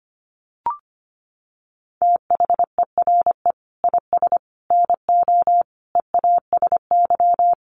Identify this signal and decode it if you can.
Morse code, heard as a keyed tone, about 25 wpm, 700 Hz. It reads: THERE IS NO EASY
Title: Sentences from Top 1300 Words 25wpm